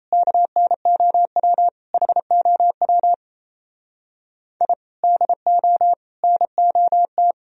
Morse code, heard as a keyed tone, about 35 wpm, 700 Hz. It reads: KNOW HOW I DO NOT